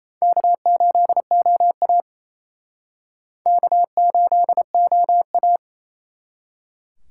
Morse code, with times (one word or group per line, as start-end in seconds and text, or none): 0.2-2.0 K8OA
3.5-5.6 K8OA